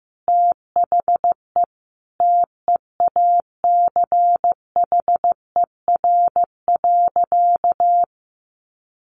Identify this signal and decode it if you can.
Morse code, heard as a keyed tone, about 15 wpm, 700 Hz. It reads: THE TEACHER.
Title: Sentences from Top 900 Words 15wpm